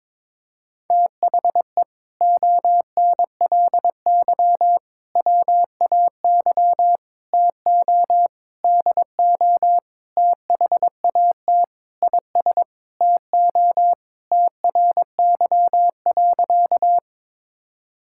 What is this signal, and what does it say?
Morse code, heard as a keyed tone, about 21 wpm, 700 Hz. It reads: THE ONLY WAY TO DO THAT IS TO TRY.
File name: practice / 22wpm / Sentences with 8-Word Phrases - 22wpm